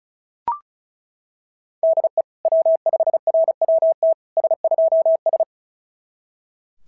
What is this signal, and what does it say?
Morse code, heard as a keyed tone, about 36 wpm, 650 Hz. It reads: DE W5RWT S2S
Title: SOTA QSO Practice 35wpm